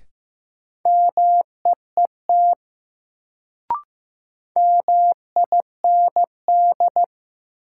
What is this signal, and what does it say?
Morse code, heard as a keyed tone, about 15 wpm, 700 Hz. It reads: MEET MIND
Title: Sets of 5 Words - New Word Drill - 15wpm